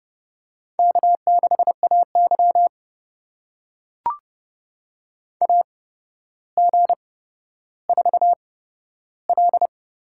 Morse code, heard as a keyed tone, about 30 wpm, spaced as for 13 wpm, 700 Hz. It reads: K6AY AG4L
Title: USA Call Signs - 8x Character Spacing - 30wpm